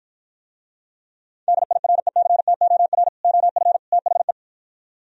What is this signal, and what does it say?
Morse code, heard as a keyed tone, about 52 wpm, 700 Hz. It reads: DIRECTOR OF THE